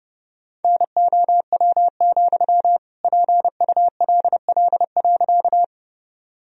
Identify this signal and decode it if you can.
Morse code, heard as a keyed tone, about 32 wpm, 700 Hz. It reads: NOW, PULL.
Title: Sentences from Top 1000 Words - Review Part 2 - 30wpm